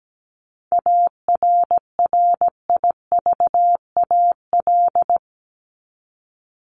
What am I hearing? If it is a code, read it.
Morse code, heard as a keyed tone, about 17 wpm, 700 Hz. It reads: ARRIVAL
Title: Top 1751-2000 Words 17wpm